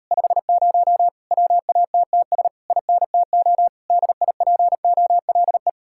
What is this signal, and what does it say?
Morse code, heard as a keyed tone, about 38 wpm, 700 Hz. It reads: 50 WATTS INTO DIPOLE